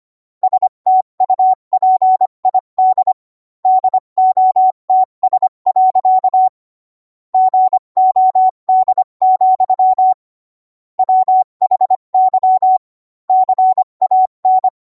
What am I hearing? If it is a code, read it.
Morse code, heard as a keyed tone, about 25 wpm, 750 Hz. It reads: STUPID DOTS. GOD, WHY CAN